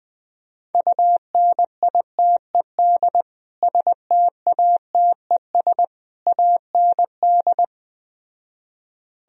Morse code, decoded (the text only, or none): UNITED STATES AND